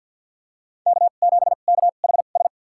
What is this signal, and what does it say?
Morse code, heard as a keyed tone, about 52 wpm, spaced as for 36 wpm, 700 Hz. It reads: K7KHS
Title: USA Call Signs - 5 Characters - 2x Character Spacing - 50wpm